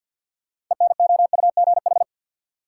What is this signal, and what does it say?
Morse code, heard as a keyed tone, about 52 wpm, 700 Hz. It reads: ENOUGH
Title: Sets of 2 Words - New Word Drill - 50wpm